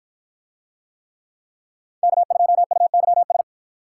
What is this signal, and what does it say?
Morse code, heard as a keyed tone, about 51 wpm, 700 Hz. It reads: K2UYS